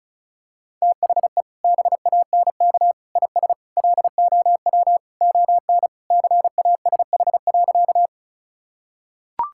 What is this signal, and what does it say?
Morse code, heard as a keyed tone, about 36 wpm, 700 Hz. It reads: THE BANK IS LOW ON CASH.
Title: Sentences from Top 1000 Words - Review Part 5 - 35wpm